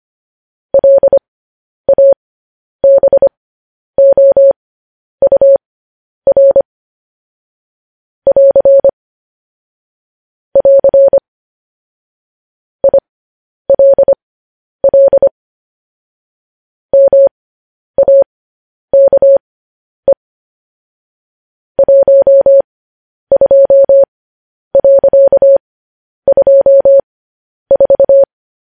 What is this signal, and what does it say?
Morse code, heard as a keyed tone, about 25 wpm, spaced as for 10 wpm, 550 Hz. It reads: LABOUR <AR> <AR> ILL MAKE 12.24
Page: news-headlines-in-morse-code-at-10-wpm